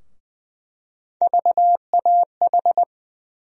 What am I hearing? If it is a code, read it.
Morse code, heard as a keyed tone, about 20 wpm, 700 Hz. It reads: VAH
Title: USA Call Sign Postfixes - 3 Letters 20wpm